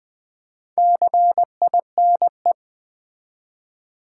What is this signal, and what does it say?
Morse code, heard as a keyed tone, about 21 wpm, 700 Hz. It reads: CINE
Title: Build A Word - 2 Letters At A Time - 6-Letter Words - 20wpm